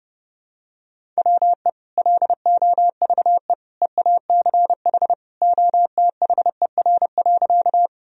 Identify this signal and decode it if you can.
Morse code, heard as a keyed tone, about 30 wpm, 700 Hz. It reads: WE LOVE EACH OTHER.
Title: Sentences from Top 1000 Words - Review Part 1 - 30wpm